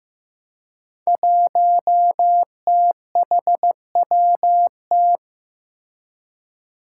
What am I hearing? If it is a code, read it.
Morse code, heard as a keyed tone, about 15 wpm, 700 Hz. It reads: 1THWT
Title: Five Challenging Letter-Number Combinations 15wpm